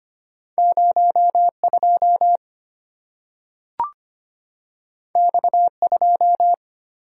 Morse code, heard as a keyed tone, about 25 wpm, 700 Hz. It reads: 02 X2
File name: Two Letter-Number Combinations - 25wpm